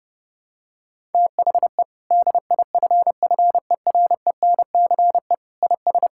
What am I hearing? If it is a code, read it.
Morse code, heard as a keyed tone, about 30 wpm, 700 Hz. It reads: THE DIFFERENCE IS
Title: Sentences from Top 1000 Words - Review Part 5 - 30wpm